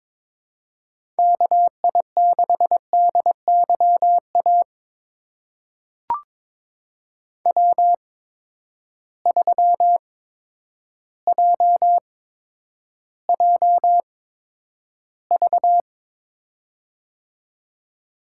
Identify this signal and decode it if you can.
Morse code, heard as a keyed tone, about 22 wpm, spaced as for 10 wpm, 700 Hz. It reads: KI6DYA W3JJV
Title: USA Call Signs - 8x Character Spacing - 22wpm